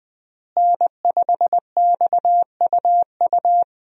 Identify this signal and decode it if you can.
Morse code, heard as a keyed tone, about 21 wpm, 700 Hz. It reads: N5XUU
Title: USA Call Signs - 5 Characters - Repeated 2x - 20wpm